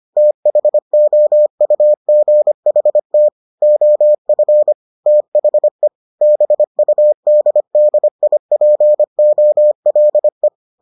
Morse code, heard as a keyed tone, about 25 wpm, 600 Hz. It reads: THOUGHT OF THE BUDDIPOLE